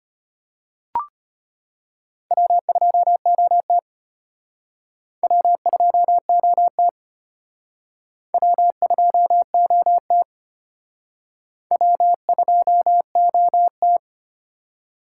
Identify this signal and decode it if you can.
Morse code, heard as a keyed tone, about 30 wpm, 700 Hz. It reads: W2OT W2OT W2OT W2OT
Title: Speed Racing - USA 1x2 Call Signs 25wpm